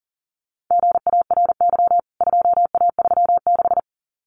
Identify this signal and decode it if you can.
Morse code, heard as a keyed tone, about 40 wpm, 700 Hz. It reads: GARY 2A36
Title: CWT Exchange 40wpm